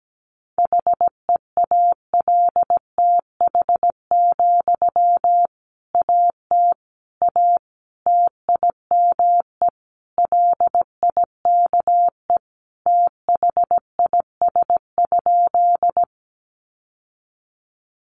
Morse code, heard as a keyed tone, about 17 wpm, 700 Hz. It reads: HEALTH, AT A TIME LIKE THIS?
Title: Sentences from Top 600 Words 17wpm